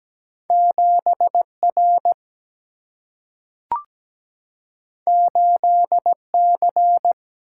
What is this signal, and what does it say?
Morse code, heard as a keyed tone, about 17 wpm, 700 Hz. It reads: 7R 8C
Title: Single-Letter Single-Number 17wpm